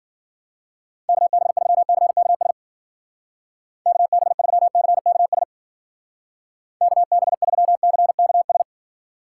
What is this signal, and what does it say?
Morse code, heard as a keyed tone, about 54 wpm, 700 Hz. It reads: KB3CKS KB3CKS KB3CKS